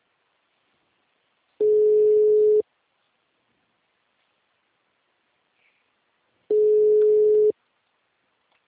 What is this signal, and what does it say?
Ringback tone (Europe, 425 Hz).
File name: out- -2019- - - .68667